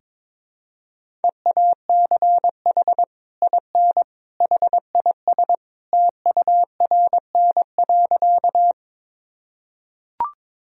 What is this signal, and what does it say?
Morse code, heard as a keyed tone, about 21 wpm, 700 Hz. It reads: EACH IN HIS TURN.
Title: Sentences from Top 1000 Words - Review Part 1 - 22wpm